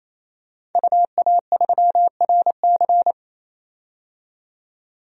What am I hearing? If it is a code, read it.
Morse code, heard as a keyed tone, about 28 wpm, 700 Hz. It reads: UA3RC
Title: DXCC-100 Call Signs - 5 Characters - 28wpm